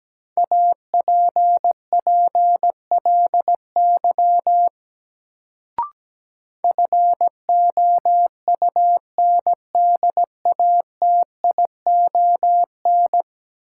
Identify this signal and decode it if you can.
Morse code, heard as a keyed tone, about 17 wpm, 700 Hz. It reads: APPLY FOUNDATION
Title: Sentences from Top 1200 Words - New Word Drill - 17wpm